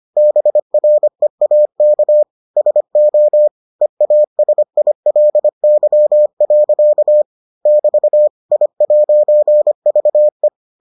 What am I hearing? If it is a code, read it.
Morse code, heard as a keyed tone, about 25 wpm, 600 Hz. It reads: BREAK SO EASILY. <BT> I'VE